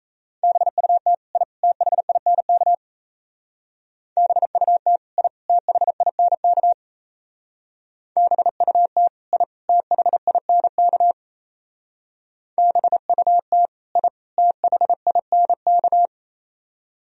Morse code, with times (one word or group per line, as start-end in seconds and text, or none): 0.4-1.1 BUT
1.4-1.4 I
1.6-2.7 THINK
4.2-5.0 BUT
5.2-5.3 I
5.5-6.7 THINK
8.2-9.1 BUT
9.3-9.4 I
9.7-11.1 THINK
12.6-13.7 BUT
14.0-14.1 I
14.4-16.1 THINK